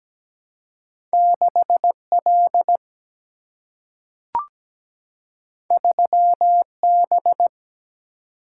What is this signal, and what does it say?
Morse code, heard as a keyed tone, about 17 wpm, 700 Hz. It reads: 6L 3B
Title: Single-Letter Single-Number 17wpm